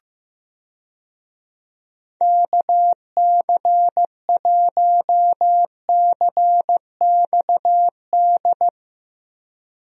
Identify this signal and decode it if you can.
Morse code, heard as a keyed tone, about 15 wpm, 700 Hz. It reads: KC1CXD
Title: USA Call Signs - 6 Characters - 15wpm